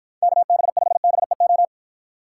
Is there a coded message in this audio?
Yes: Morse code from 0.2 to 1.7 s, heard as a keyed tone, about 53 wpm, 700 Hz. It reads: KB5BEO